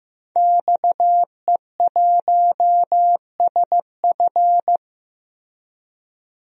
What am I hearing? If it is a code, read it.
Morse code, heard as a keyed tone, about 16 wpm, 700 Hz. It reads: XE1SF
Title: DXCC-100 Call Signs - 5 Characters - Repeated 3x - 15wpm